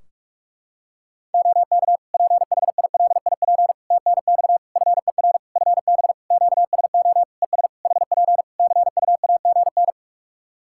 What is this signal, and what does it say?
Morse code, heard as a keyed tone, about 42 wpm, 700 Hz. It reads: OK PHILIP TNX FER FB QSO ES HP CUAGN
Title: K1USN SST QSO Practice 45wpm